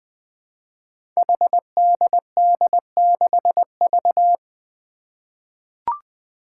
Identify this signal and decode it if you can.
Morse code, heard as a keyed tone, about 21 wpm, 700 Hz. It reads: HDD6V
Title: Five Challenging Letter-Number Combinations 20wpm